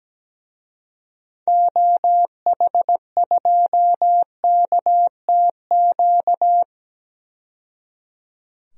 Morse code, heard as a keyed tone, about 17 wpm, 700 Hz. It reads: OH2KTQ
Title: DXCC-100 Call Signs - 6 Characters - 17wpm